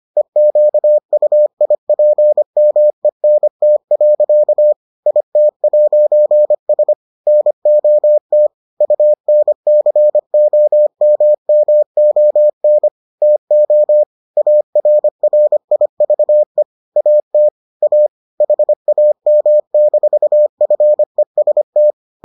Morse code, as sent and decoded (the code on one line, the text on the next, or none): . --.- ..- .. .--. -- . -. - .-.-.- / .. - .----. ... / -. --- - / ..- -. -.-. --- -- -- --- -. / - --- / .- .-. .-. .. ...- . / .- - / .- / .... .- -- -....- ..-. ... -
EQUIPMENT. IT'S NOT UNCOMMON TO ARRIVE AT A HAM-FEST